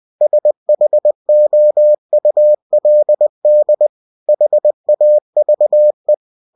Morse code, heard as a keyed tone, about 20 wpm, 600 Hz. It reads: SHOULD HAVE
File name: cw-books / winnie-the-pooh / winnie-20wpm-page-0090